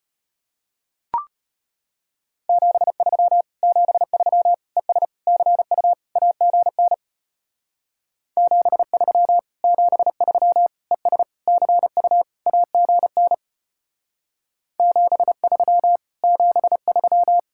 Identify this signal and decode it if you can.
Morse code, heard as a keyed tone, about 32 wpm, 690 Hz. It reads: 73 73 ES CU AGN 73 73 ES CU AGN 73 73